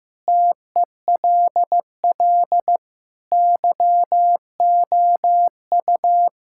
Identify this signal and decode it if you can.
Morse code, heard as a keyed tone, about 15 wpm, 700 Hz. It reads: TELL YOU
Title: Sentences from Top 1000 Words - Review Part 6 - 15wpm